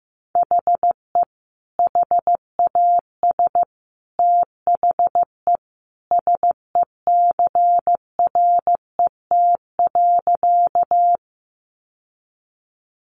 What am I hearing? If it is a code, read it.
Morse code, heard as a keyed tone, about 15 wpm, 700 Hz. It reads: HE HAS THE SECRET.
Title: Sentences from Top 1200 Words 15wpm